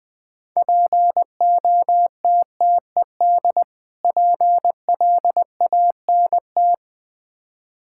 Morse code, heard as a keyed tone, about 20 wpm, 700 Hz. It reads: POTTED PLANT
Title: Difficult Words with Hints - 20wpm